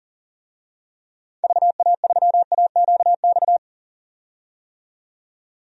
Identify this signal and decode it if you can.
Morse code, heard as a keyed tone, about 37 wpm, 700 Hz. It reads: VA3AQX
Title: DXCC-100 Call Signs - 6 Characters - 40wpm